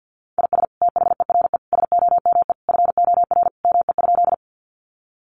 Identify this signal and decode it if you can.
Morse code, heard as a keyed tone, about 48 wpm, 700 Hz. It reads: IS THERE SOME FOR ME?
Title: Sentences from Top 200 Words 50wpm